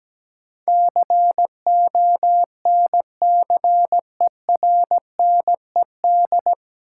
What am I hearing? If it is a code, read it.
Morse code, heard as a keyed tone, about 17 wpm, 700 Hz. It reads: CONCERNED
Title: Sentences from Top 1000 Words - Review Part 6 - New Word Drill - 17wpm